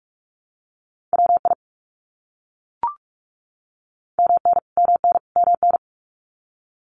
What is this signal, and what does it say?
Morse code, heard as a keyed tone, about 44 wpm, 700 Hz. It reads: WI MN MN MN
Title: US States Repeated 3x 45wpm